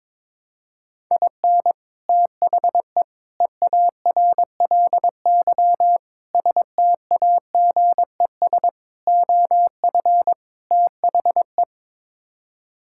Morse code, heard as a keyed tone, about 22 wpm, 700 Hz. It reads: IN THE EARLY STAGES OF THE
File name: practice / 22wpm / Sets of 6 Words - 22wpm